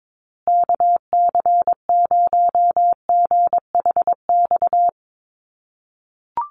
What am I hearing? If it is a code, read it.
Morse code, heard as a keyed tone, about 22 wpm, 700 Hz. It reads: KC0GHX
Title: USA Call Signs Repeated 3x 22wpm